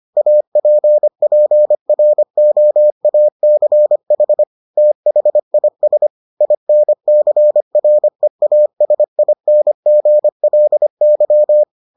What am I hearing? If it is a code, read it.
Morse code, heard as a keyed tone, about 25 wpm, 600 Hz. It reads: APPROACH THIS INCREASINGLY